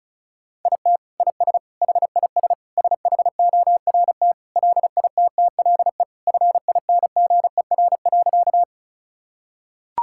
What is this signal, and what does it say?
Morse code, heard as a keyed tone, about 34 wpm, 700 Hz. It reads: IT IS HIS SHORT LITTLE FINGER.
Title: Sentences from Top 1600 Words 35wpm